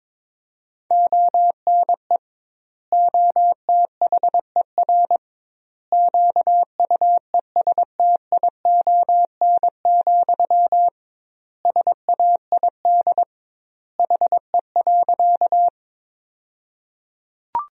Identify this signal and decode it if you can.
Morse code, heard as a keyed tone, about 22 wpm, 700 Hz. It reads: ONE OTHER QUESTION, SAID HE.
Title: 2x Word-Spacing - Sentences from Top 400 Words - 22wpm